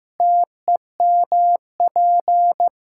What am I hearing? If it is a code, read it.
Morse code, heard as a keyed tone, about 15 wpm, 700 Hz. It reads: TEMP
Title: Practice QSO - Exchange 2 - Info - 15wpm